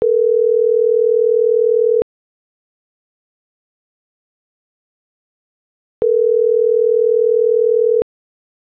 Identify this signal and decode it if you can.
Ringback tone (North America), 440 + 480 Hz.